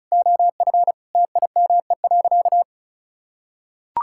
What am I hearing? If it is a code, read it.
Morse code, heard as a keyed tone, about 36 wpm, 700 Hz. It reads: OF TIME.
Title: Sentences from Top 1000 Words - Review Part 5 - 35wpm